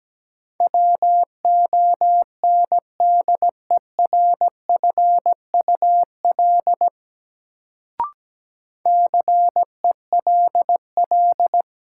Morse code, heard as a keyed tone, about 17 wpm, 700 Hz. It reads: WONDERFUL CELL
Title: Sentences from Top 1100 Words - New Word Drill - 17wpm